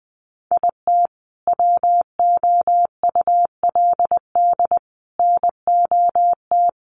Morse code, heard as a keyed tone, about 19 wpm, 700 Hz. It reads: IT WOULD NOT
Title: Sentences from Top 1500 Words 20wpm